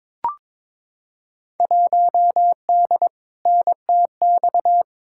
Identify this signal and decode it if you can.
Morse code, heard as a keyed tone, about 22 wpm, 700 Hz. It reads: 1D NTX
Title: Field Day Exchanges - 22wpm